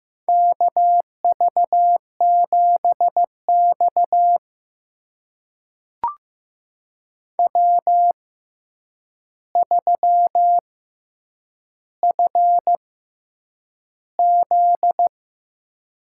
Morse code, heard as a keyed tone, about 15 wpm, spaced as for 8 wpm, 700 Hz. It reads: KV7X W3FZ
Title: USA Call Signs - 4 Characters - 6x Character Spacing - 15wpm